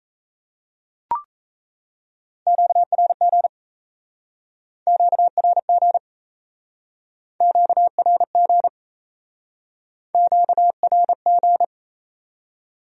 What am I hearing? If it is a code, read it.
Morse code, heard as a keyed tone, about 37 wpm, 700 Hz. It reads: QRG QRG QRG QRG